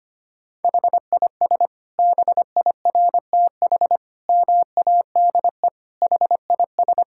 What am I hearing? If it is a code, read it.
Morse code, heard as a keyed tone, about 25 wpm, 700 Hz. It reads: HIS BIRTH MADE HIS
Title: Sentences from Top 1000 Words - Review Part 5 - 25wpm